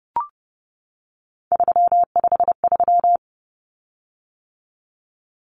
Morse code, heard as a keyed tone, about 30 wpm, 700 Hz. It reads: 353
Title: Common Cut Numbers - 3-Digit 30wpm